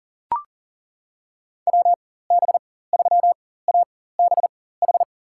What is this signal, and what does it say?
Morse code, heard as a keyed tone, about 40 wpm, spaced as for 19 wpm, 700 Hz. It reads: WB3ABH
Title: USA Call Signs - 4x Character Spacing - 40wpm